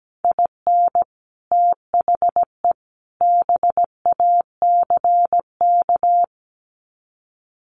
Morse code, heard as a keyed tone, about 17 wpm, 700 Hz. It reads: IN THE BACK
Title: Sets of 3 Words 17wpm